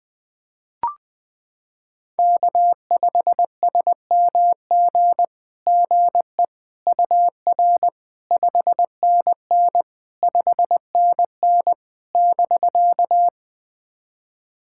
Morse code, heard as a keyed tone, about 21 wpm, 700 Hz. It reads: K5SMG GE UR 5NN 5NN <BK>
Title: SOTA QSO Practice 20wpm